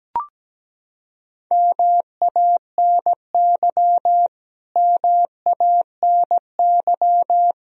Morse code, heard as a keyed tone, about 17 wpm, 700 Hz. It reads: MANY MANY
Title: Top 100 Words Repeated 17wpm